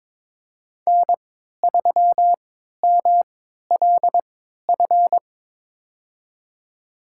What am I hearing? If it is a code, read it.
Morse code, heard as a keyed tone, about 22 wpm, spaced as for 13 wpm, 700 Hz. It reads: N3MLF
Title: USA Call Signs - 5 Characters - 3x Character Spacing - 22wpm